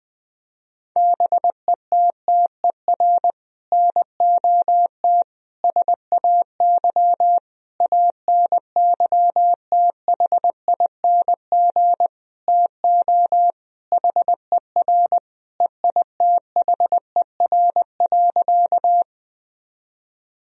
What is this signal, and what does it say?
Morse code, heard as a keyed tone, about 20 wpm, 700 Hz. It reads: BETTER NOT SAY ANYTHING TO HER EITHER.
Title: Sentences from Top 1000 Words - Review Part 4 - 20wpm